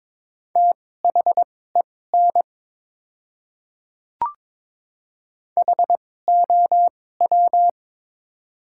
Morse code, heard as a keyed tone, about 22 wpm, spaced as for 16 wpm, 700 Hz. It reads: THEN HOW
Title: 2x Character Spacing - Top 200 Words - 22wpm